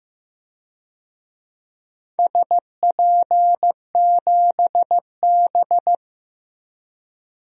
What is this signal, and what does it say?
Morse code, heard as a keyed tone, about 15 wpm, 700 Hz. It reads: SP7B
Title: DXCC-100 Call Signs - 4 Characters - 15wpm